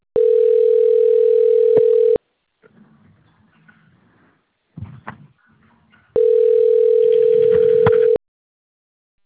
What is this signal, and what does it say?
Ringback tone (North America), 440 + 480 Hz.
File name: out- - - - - .11048